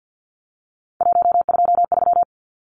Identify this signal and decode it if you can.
Morse code, heard as a keyed tone, about 50 wpm, 700 Hz. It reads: 123
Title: Numbers - 3-Digit 50wpm